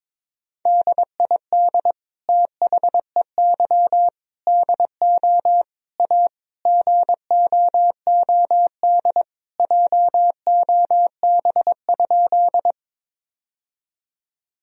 Morse code, heard as a keyed tone, about 22 wpm, 700 Hz. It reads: DID THEY DO A GOOD JOB?
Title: Sentences from Top 1000 Words - Review Part 3 - 22wpm